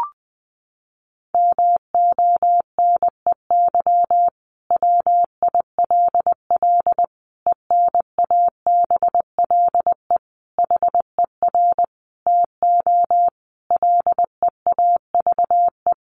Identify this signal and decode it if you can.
Morse code, heard as a keyed tone, about 20 wpm, 700 Hz. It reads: MONEY WILL ENABLE HER TO LEAVE
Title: Sentences from Top 1500 Words 20wpm